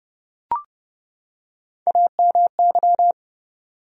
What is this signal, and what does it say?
Morse code, heard as a keyed tone, about 30 wpm, 700 Hz. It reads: AMY
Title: USA Common Names - Max 4 Letters - 30wpm